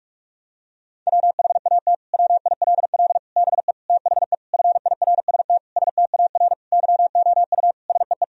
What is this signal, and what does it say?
Morse code, heard as a keyed tone, about 42 wpm, 700 Hz. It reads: WHAT WILL BE THE FIRST STAR YOU SEE
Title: Sentences from Top 1000 Words - Review Part 2 - 45wpm